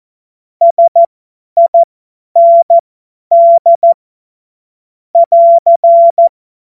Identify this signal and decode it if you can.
Morse code, heard as a keyed tone, about 14 wpm, spaced as for 10 wpm, 680 Hz. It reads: SIND <AR>
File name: Witz-21-Oktober-2025-50-F70